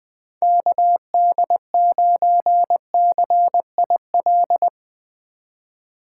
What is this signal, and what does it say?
Morse code, heard as a keyed tone, about 21 wpm, 700 Hz. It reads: KD9CIL